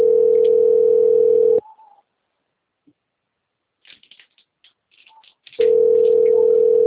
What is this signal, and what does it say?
Ringback tone (North America), 440 + 480 Hz.
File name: rg-1600-unknown- - - .90037